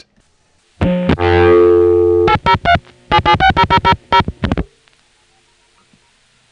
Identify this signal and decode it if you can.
DTMF, heard as telephone touch tones, tones of about 90 ms. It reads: 803 7760000